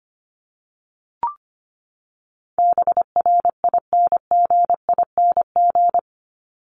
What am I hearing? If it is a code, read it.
Morse code, heard as a keyed tone, about 25 wpm, 700 Hz. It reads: BRINGING
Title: ING Suffix - Top 100 Words with Base Words 25wpm